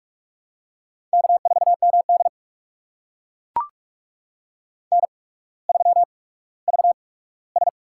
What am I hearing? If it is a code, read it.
Morse code, heard as a keyed tone, about 44 wpm, spaced as for 19 wpm, 690 Hz. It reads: K4MD N3VS
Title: USA Call Signs - 4 Characters - 8x Character Spacing - 45wpm